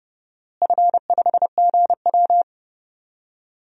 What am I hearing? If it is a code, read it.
Morse code, heard as a keyed tone, about 30 wpm, 700 Hz. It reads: F5GW
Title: DXCC-100 Call Signs - 4 Characters - Repeated 2x - 30wpm